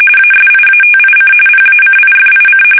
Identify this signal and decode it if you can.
RTTY, about 45.45 baud. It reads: ASIATIC PACIFIC
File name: fsk850 / battleshipnj